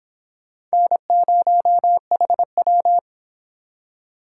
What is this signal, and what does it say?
Morse code, heard as a keyed tone, about 31 wpm, 700 Hz. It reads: N0HW